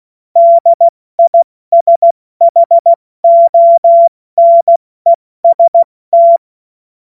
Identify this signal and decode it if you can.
Morse code, heard as a keyed tone, about 16 wpm, 680 Hz. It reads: DISHONEST
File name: Morsekurs / Witze / Witz 27. September 2022 70 F80